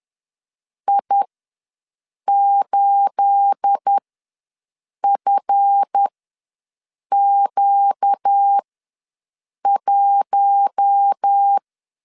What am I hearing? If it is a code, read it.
Morse code, heard as a keyed tone, about 10 wpm, spaced as for 6 wpm, 800 Hz. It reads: I8FQ1
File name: LEZIONI / 9 286 ws 10wpm 4032